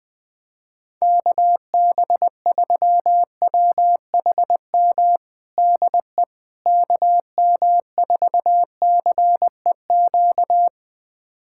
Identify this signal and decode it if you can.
Morse code, heard as a keyed tone, about 20 wpm, 700 Hz. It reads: KB3WHM DE KM4CEQ
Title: USA Callsign DE Callsign - 6 Characters - 20wpm